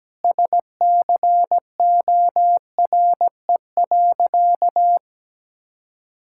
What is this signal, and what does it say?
Morse code, heard as a keyed tone, about 17 wpm, 700 Hz. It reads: SCORE.
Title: Sentences from Top 1600 Words 17wpm